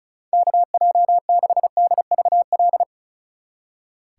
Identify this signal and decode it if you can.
Morse code, heard as a keyed tone, about 46 wpm, 700 Hz. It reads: KJ6DVL